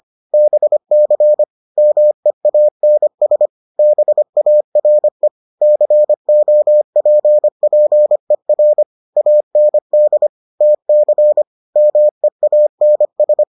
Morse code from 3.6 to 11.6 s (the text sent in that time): BARE COPPER AND TC